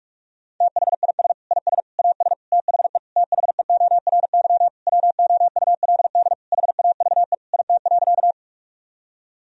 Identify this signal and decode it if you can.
Morse code, heard as a keyed tone, about 46 wpm, 700 Hz. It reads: THIS IS AS THE THEORY WOULD HAVE IT.
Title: Sentences from Top 1000 Words - Review Part 3 - 45wpm